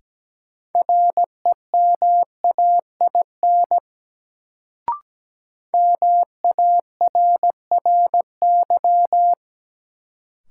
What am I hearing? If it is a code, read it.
Morse code, heard as a keyed tone, about 17 wpm, 700 Hz. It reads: REMAIN MARRY
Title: Sentences from Top 1200 Words - New Word Drill Alt - 17wpm